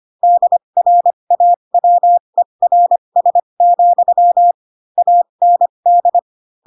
Morse code, heard as a keyed tone, about 25 wpm, spaced as for 22 wpm, 700 Hz. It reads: DRAWERS, AND